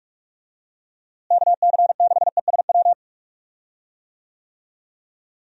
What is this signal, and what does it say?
Morse code, heard as a keyed tone, about 47 wpm, 700 Hz. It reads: KC6ESW
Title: USA Call Signs - 6 Characters - Repeated 2x - 45wpm